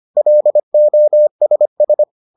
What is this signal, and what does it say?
Morse code, heard as a keyed tone, about 25 wpm, 600 Hz. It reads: LOSS